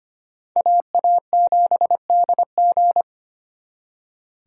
Morse code, heard as a keyed tone, about 32 wpm, 700 Hz. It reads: AA7DG